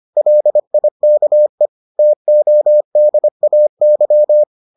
Morse code, heard as a keyed tone, about 25 wpm, 600 Hz. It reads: LIKE TODAY